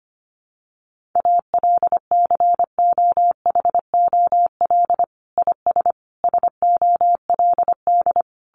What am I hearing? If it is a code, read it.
Morse code, heard as a keyed tone, about 25 wpm, 700 Hz. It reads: ALCOHOL IS SOLD
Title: Sentences from Top 1400 Words 25wpm